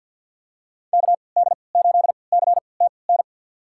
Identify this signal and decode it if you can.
Morse code, heard as a keyed tone, about 46 wpm, spaced as for 27 wpm, 700 Hz. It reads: KD8CTN
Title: USA Call Signs - 6 Characters - 3x Character Spacing - 50wpm